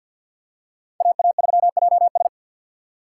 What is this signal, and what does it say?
Morse code, heard as a keyed tone, about 48 wpm, 700 Hz. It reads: AA3JS
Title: USA Call Signs - 5 Characters - 4x Character Spacing - 50wpm